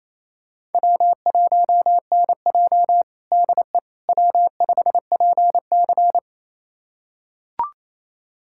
Morse code, heard as a keyed tone, about 28 wpm, 700 Hz. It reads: W1NJ DE W5PC
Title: USA Callsign DE Callsign - 4 Characters - 28wpm